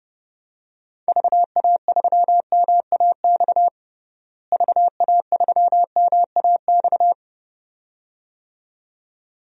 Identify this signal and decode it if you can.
Morse code, heard as a keyed tone, about 30 wpm, 700 Hz. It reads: VA3MAX VA3MAX
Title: DXCC-100 Call Signs - 6 Characters - Repeated 2x - 30wpm